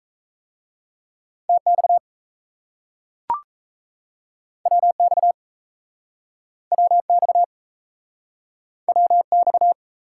Morse code, heard as a keyed tone, about 38 wpm, 700 Hz. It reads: TX WX WX WX